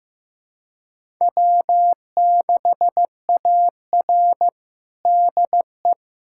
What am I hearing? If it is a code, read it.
Morse code, heard as a keyed tone, about 16 wpm, 700 Hz. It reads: W6AR DE
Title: USA Callsign DE Callsign - 4 Characters - 15wpm